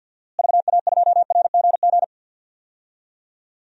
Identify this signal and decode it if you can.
Morse code, heard as a keyed tone, about 52 wpm, 700 Hz. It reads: VA2RGG
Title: DXCC-100 Call Signs - 6 Characters - Repeated 2x - 50wpm